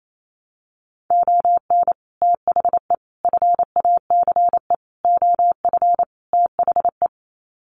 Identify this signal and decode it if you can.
Morse code, heard as a keyed tone, about 28 wpm, 700 Hz. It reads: ON THE FACE OF THE
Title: Sets of 5 Words Encore2 28wpm